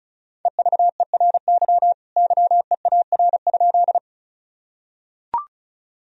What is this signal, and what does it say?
Morse code, heard as a keyed tone, about 36 wpm, 700 Hz. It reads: EVERY YEAR?
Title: Sentences from Top 1000 Words - Review Part 1 - 35wpm